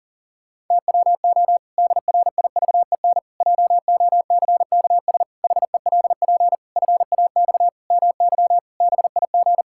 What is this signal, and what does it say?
Morse code, heard as a keyed tone, about 40 wpm, 700 Hz. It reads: TWO DRIVEN JOCKS HELP FAX MY BIG
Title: Pangram Challenge v2 - 40wpm